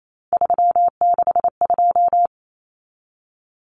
Morse code, heard as a keyed tone, about 28 wpm, 700 Hz. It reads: 362
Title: Common Cut Numbers - 3-Digit 28wpm